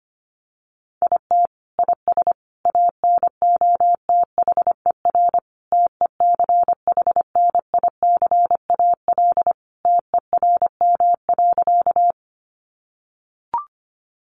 Morse code, heard as a keyed tone, about 25 wpm, 700 Hz. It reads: IT IS ANOTHER TECHNICAL TERM.
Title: Sentences from Top 1100 Words 25wpm